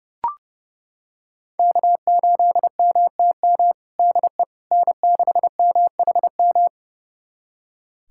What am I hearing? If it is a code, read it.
Morse code, heard as a keyed tone, about 30 wpm, 700 Hz. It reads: K8MTM DE N6MHM
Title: USA Callsign DE Callsign - 5 Characters - 30wpm